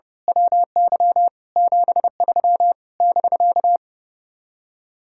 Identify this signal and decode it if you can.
Morse code, heard as a keyed tone, about 29 wpm, 700 Hz. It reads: WY 73 <BK>
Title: Speed-Racing Common QSO Phrases - 30wpm